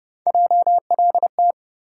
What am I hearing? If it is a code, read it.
Morse code, heard as a keyed tone, about 30 wpm, 700 Hz. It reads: JLT
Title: USA Call Sign Postfixes - 3 Letters 30wpm